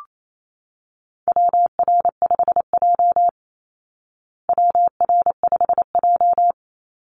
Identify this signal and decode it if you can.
Morse code, heard as a keyed tone, about 28 wpm, 700 Hz. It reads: WR5J WR5J
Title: Call Signs Repeated 2x - 28wpm